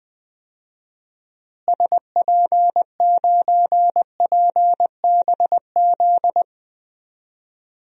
Morse code, heard as a keyed tone, about 20 wpm, 700 Hz. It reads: SP9PBZ
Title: DXCC-100 Call Signs - 6 Characters - Repeated 2x - 20wpm